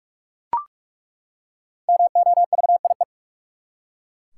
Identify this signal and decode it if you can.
Morse code, heard as a keyed tone, about 43 wpm, 700 Hz. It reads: MOVIE
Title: Five Letter Words - 45wpm